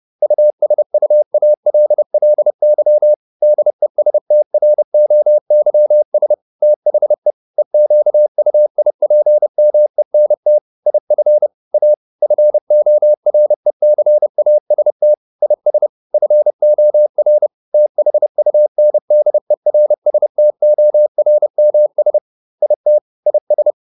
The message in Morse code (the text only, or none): USUALLY DESTROYS THE EQUIPMENT IF A FORECAST IS FOR THUNDERSTORMS IT IS